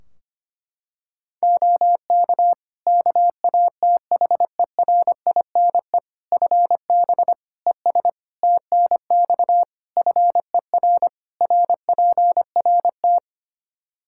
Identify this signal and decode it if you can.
Morse code, heard as a keyed tone, about 25 wpm, 700 Hz. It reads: OK KATHERINE FB ES TNX FER RPRT
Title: K1USN SST QSO Practice 25wpm